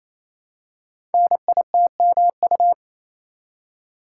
Morse code, heard as a keyed tone, about 27 wpm, 700 Hz. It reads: NITMU
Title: Code Groups - 5 Letter:Numbers - 28wpm